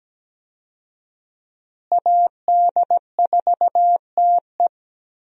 Morse code, heard as a keyed tone, about 17 wpm, 700 Hz. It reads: AD4TE
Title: USA Call Signs - 5 Characters - Repeated 2x - 17wpm